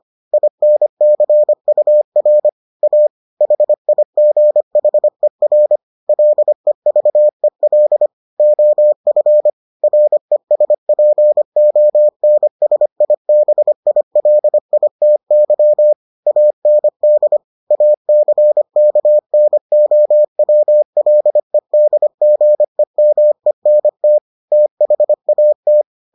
Morse code, heard as a keyed tone, about 25 wpm, 600 Hz. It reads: INCUR A HIGHER LEVEL OF RESPONSIBILITY AND ACKNOWLEDGEMENT THAT